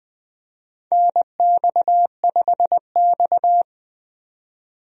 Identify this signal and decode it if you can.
Morse code, heard as a keyed tone, about 20 wpm, 700 Hz. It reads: NX5X
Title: USA Call Signs - 4 Characters - 20wpm